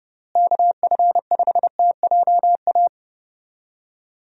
Morse code, heard as a keyed tone, about 29 wpm, 700 Hz. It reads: KF5TJA
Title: USA Call Signs - 6 Characters - Repeated 2x - 30wpm